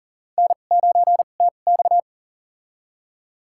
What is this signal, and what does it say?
Morse code, heard as a keyed tone, about 40 wpm, spaced as for 29 wpm, 700 Hz. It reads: N9TX
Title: USA Call Signs - 4 Characters - 2x Character Spacing - 40wpm